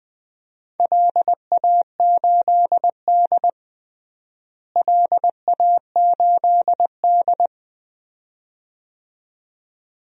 Morse code, heard as a keyed tone, about 20 wpm, 700 Hz. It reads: LA8D LA8D
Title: DXCC-100 Call Signs - 4 Characters - Repeated 2x - 20wpm